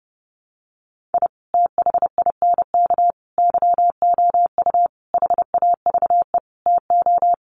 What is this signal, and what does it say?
Morse code, heard as a keyed tone, about 30 wpm, 700 Hz. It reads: I THINK YOU HAVE TO
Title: Sets of 5 Words 30wpm